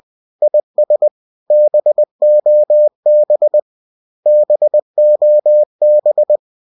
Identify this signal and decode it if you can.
Morse code, heard as a keyed tone, about 20 wpm, 600 Hz. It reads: IS BOB BOB